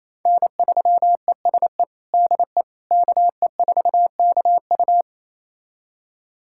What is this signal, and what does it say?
Morse code, heard as a keyed tone, about 27 wpm, 700 Hz. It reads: N3ESE DE KE4KU
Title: USA Callsign DE Callsign - 5 Characters - 28wpm